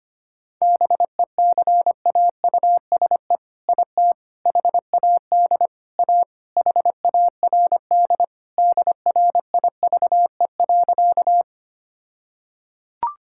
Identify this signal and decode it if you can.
Morse code, heard as a keyed tone, about 25 wpm, 700 Hz. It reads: BECAUSE IT HAD A HARD DRIVE.
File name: practice / 25wpm / All-Age Appropriate Jokes - 25wpm